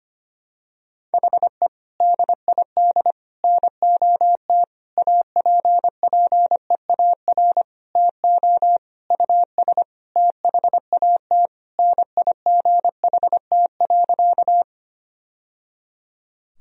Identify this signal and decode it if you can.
Morse code, heard as a keyed tone, about 25 wpm, 700 Hz. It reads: HE DID NOT APPEAR TO US THAT NIGHT.
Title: Sentences from Top 1000 Words - Review Part 2 - 25wpm